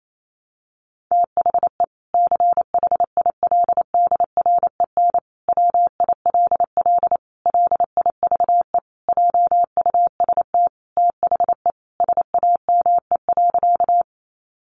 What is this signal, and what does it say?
Morse code, heard as a keyed tone, about 28 wpm, 700 Hz. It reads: THE CHILDREN WILL LIVE JUST THE SAME.
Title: Sentences from Top 300 Words 28wpm